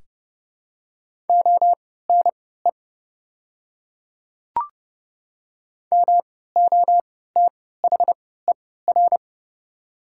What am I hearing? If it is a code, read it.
Morse code, heard as a keyed tone, about 30 wpm, spaced as for 17 wpm, 700 Hz. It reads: ONE MOTHER
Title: 3x Character Spacing - Top 200 Words - 30wpm